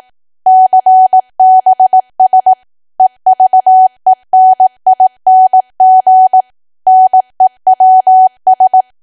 Morse code, heard as a keyed tone, about 18 wpm, 750 Hz. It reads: CBS EVENING NEWS
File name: Archive / 18 WPM / 18WPM